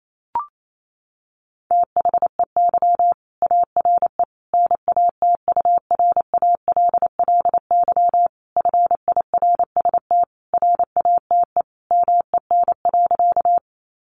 Morse code, heard as a keyed tone, about 28 wpm, 700 Hz. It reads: THEY ARE NATURALLY FIRST RATE MEN.
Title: Sentences from Top 1300 Words 28wpm